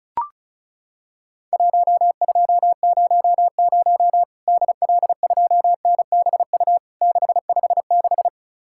Morse code, heard as a keyed tone, about 36 wpm, 700 Hz. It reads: 1200 DL2NBU 656
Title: Worked All Europe DX Contest - QTC Exchange - 35wpm